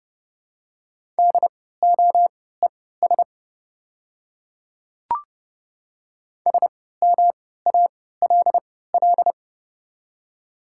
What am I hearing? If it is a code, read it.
Morse code, heard as a keyed tone, about 31 wpm, spaced as for 17 wpm, 700 Hz. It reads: DOES SMALL
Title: 3x Character Spacing - Top 300 Words - 30wpm